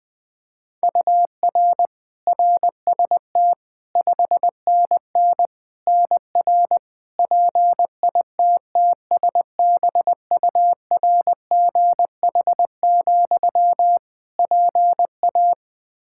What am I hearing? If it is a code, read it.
Morse code, heard as a keyed tone, about 21 wpm, 700 Hz. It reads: UR RST 5NN NR PITTSBURGH, PA
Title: Practice QSO - Exchange 1 - Beginning - 20wpm